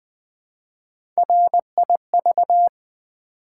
Morse code, heard as a keyed tone, about 20 wpm, 700 Hz. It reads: RIV
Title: Build A Word - 1 Letter At A Time - 5-Letter Words - 20wpm